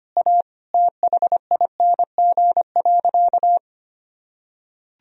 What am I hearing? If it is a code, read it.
Morse code, heard as a keyed tone, about 25 wpm, 700 Hz. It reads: A THING.
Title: Sentences from Top 200 Words and Abbreviations 25wpm